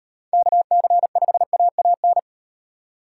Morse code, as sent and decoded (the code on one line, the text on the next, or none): -.- -.-. ..... .- .- -.
KC5AAN